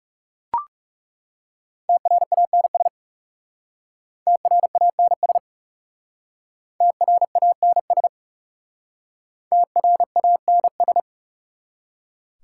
Morse code, heard as a keyed tone, about 39 wpm, 700 Hz. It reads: TRANS TRANS TRANS TRANS